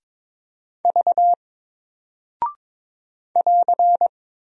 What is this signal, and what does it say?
Morse code, heard as a keyed tone, about 22 wpm, 700 Hz. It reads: V <AR>
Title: Single Character (Letter:Number:Punctuation:Prosign) Alt 22wpm